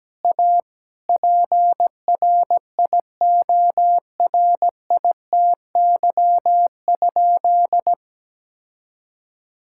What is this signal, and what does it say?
Morse code, heard as a keyed tone, about 17 wpm, 700 Hz. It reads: A PRIORITY?
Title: Sentences from Top 1600 Words 17wpm